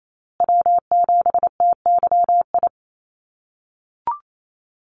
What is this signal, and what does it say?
Morse code, heard as a keyed tone, about 28 wpm, 700 Hz. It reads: W7TYI
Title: USA Call Signs Repeated 2x 28wpm